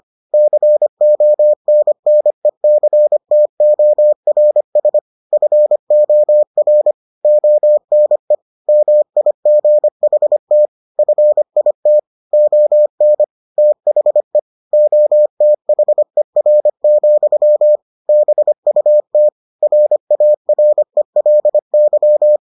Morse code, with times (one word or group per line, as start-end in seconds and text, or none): 0.3-5.0 CONNECTORS
5.3-6.9 FOR
7.2-8.4 ONE
8.7-10.7 MIGHT
11.0-12.0 FIT
12.3-13.2 ON
13.6-14.4 THE
14.7-17.8 OTHER,
18.1-19.3 BUT
19.6-22.4 RARELY